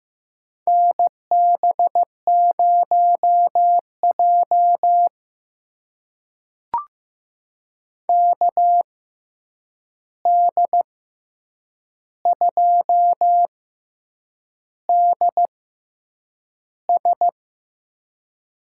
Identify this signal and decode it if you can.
Morse code, heard as a keyed tone, about 15 wpm, spaced as for 7 wpm, 700 Hz. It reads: NB0J KD2DS
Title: USA Call Signs - 6x Character Spacing - 15wpm